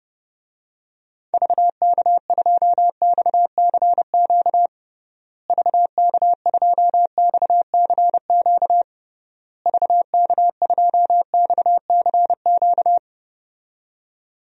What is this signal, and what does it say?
Morse code, heard as a keyed tone, about 32 wpm, 700 Hz. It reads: VK2XCQ VK2XCQ VK2XCQ
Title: DXCC-100 Call Signs - 6 Characters - Repeated 3x - 30wpm